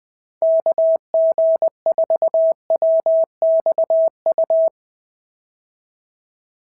Morse code, heard as a keyed tone, about 20 wpm, 650 Hz. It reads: KG4WXU
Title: POTA QSO Practice - 20wpm